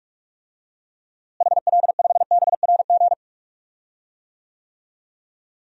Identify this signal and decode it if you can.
Morse code, heard as a keyed tone, about 46 wpm, 700 Hz. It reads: HL5BRG
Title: DXCC-100 Call Signs - 6 Characters - Repeated 3x - 45wpm